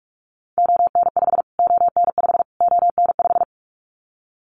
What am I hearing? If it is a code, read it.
Morse code, heard as a keyed tone, about 44 wpm, 700 Hz. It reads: ON5 ON5 ON5
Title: Callsign Prefixes for Top 25 DXCC - Repeated 3x - 45wpm